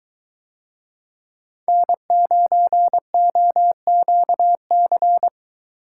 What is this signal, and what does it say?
Morse code, heard as a keyed tone, about 22 wpm, 700 Hz. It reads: N9OQC